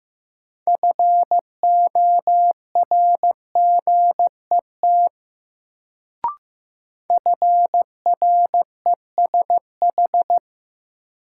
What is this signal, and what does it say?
Morse code, heard as a keyed tone, about 15 wpm, 700 Hz. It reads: FORGET FRESH
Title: Sentences from Top 1200 Words - New Word Drill - 15wpm